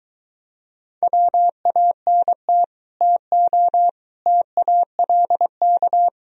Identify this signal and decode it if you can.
Morse code, heard as a keyed tone, about 24 wpm, 700 Hz. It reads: WANT TO TALK